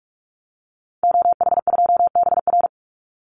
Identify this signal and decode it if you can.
Morse code, heard as a keyed tone, about 44 wpm, 700 Hz. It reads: OH2BR
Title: Call Signs Repeated 3x - 45wpm